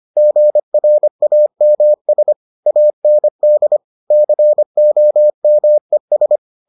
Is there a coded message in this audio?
Yes: Morse code from 0.2 to 6.4 s, heard as a keyed tone, about 25 wpm, 600 Hz. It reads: GRAMS AND COMES